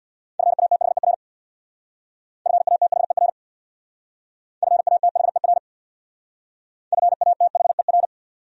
Morse code, heard as a keyed tone, about 60 wpm, 700 Hz. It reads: FATHER FATHER FATHER FATHER
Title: Speed-Racing Top 500 Words 50wpm